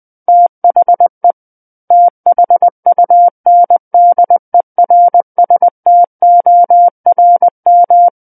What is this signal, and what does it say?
Morse code, heard as a keyed tone, about 20 wpm, 700 Hz. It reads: THE THUNDERSTORM
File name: 20 WPM